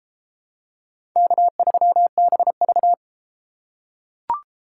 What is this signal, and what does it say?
Morse code, heard as a keyed tone, about 30 wpm, 700 Hz. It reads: K3BV